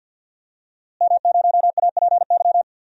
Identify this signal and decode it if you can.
Morse code, heard as a keyed tone, about 49 wpm, 700 Hz. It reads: M0APY
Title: DXCC-100 Call Signs - 5 Characters - Repeated 2x - 50wpm